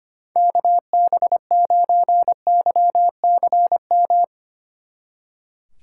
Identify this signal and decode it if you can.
Morse code, heard as a keyed tone, about 32 wpm, 700 Hz. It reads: KB9YCM